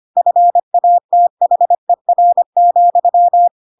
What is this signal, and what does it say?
Morse code, heard as a keyed tone, about 25 wpm, 700 Hz. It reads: FATHER,
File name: Pinocchio / Pinocchio ch7 25 25 ik1hge